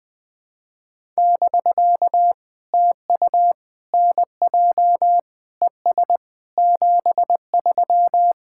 Morse code, heard as a keyed tone, about 20 wpm, 700 Hz. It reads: <BK> TU NJ ES 73
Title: POTA QSO Practice - 20wpm